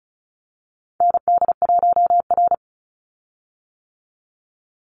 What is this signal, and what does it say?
Morse code, heard as a keyed tone, about 34 wpm, 700 Hz. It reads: ND1R
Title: Call Signs - 35wpm